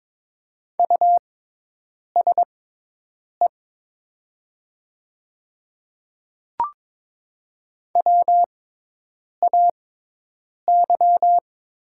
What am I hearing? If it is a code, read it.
Morse code, heard as a keyed tone, about 22 wpm, spaced as for 8 wpm, 700 Hz. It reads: USE WAY
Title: 6x Character Spacing - Top 100 Words - 22wpm